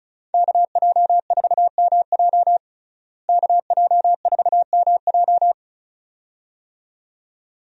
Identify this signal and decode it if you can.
Morse code, heard as a keyed tone, about 36 wpm, 700 Hz. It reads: KJ4MJ KJ4MJ
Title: USA Call Signs - 5 Characters - Repeated 2x - 35wpm